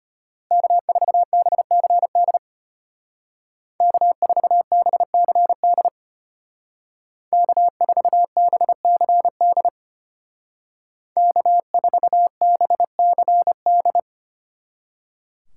Morse code, heard as a keyed tone, about 32 wpm, 700 Hz. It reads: K4BCD K4BCD K4BCD K4BCD